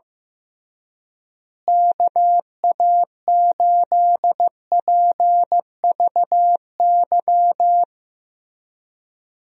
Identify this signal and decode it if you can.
Morse code, heard as a keyed tone, about 15 wpm, 700 Hz. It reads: KA8PVY
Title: USA Call Signs - 6 Characters - Repeated 3x - 15wpm